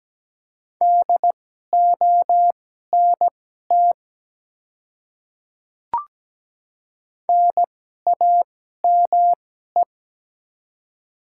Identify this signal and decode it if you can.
Morse code, heard as a keyed tone, about 17 wpm, spaced as for 12 wpm, 700 Hz. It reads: DONT NAME
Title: 2x Character Spacing - Top 300 Words - 17wpm